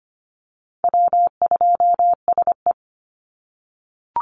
Morse code, heard as a keyed tone, about 25 wpm, 700 Hz. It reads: W2SE
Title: NA SOTA Chaser Callsigns 25wpm